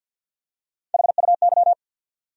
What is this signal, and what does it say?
Morse code, heard as a keyed tone, about 48 wpm, 700 Hz. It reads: HUY
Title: USA Call Sign Postfixes - 3 Letters 50wpm